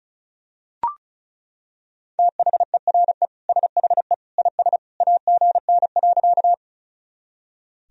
Morse code, heard as a keyed tone, about 36 wpm, 700 Hz. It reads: THERE SHE IS AGN.
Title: Sentences from Top 200 Words and Abbreviations 35wpm